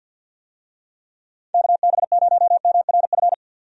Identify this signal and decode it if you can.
Morse code, heard as a keyed tone, about 48 wpm, 700 Hz. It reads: KB0MRF
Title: USA Call Signs - 6 Characters - 50wpm